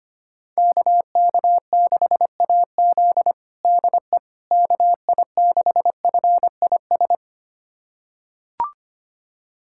Morse code, heard as a keyed tone, about 25 wpm, 700 Hz. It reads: KK6AZ DE KI6FIS
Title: USA Callsign DE Callsign 25wpm